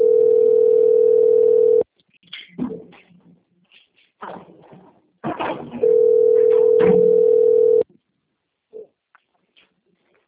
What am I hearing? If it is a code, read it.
Ringback tone (North America), 440 + 480 Hz.
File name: rg-1600-unknown- - - .90383